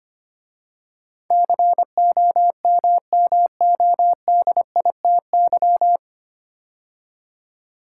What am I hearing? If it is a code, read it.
Morse code, heard as a keyed tone, about 25 wpm, 700 Hz. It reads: COMMODITY
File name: practice / 25wpm / 9 Letter Words - First 3 Letters Given - 25wpm